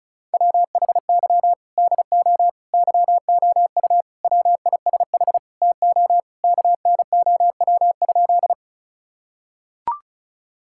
Morse code, heard as a keyed tone, about 34 wpm, 700 Hz. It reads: WHY DO YOU WISH TO KNOW?
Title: Sentences from Top 1600 Words 35wpm